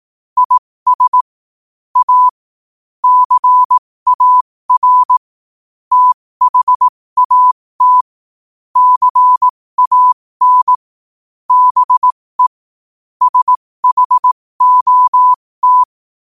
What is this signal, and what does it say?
Morse code, heard as a keyed tone, about 18 wpm, spaced as for 15 wpm, 1000 Hz. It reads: IS A CAR THAT CAN BE SHOT
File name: qotdfiles / 15wpm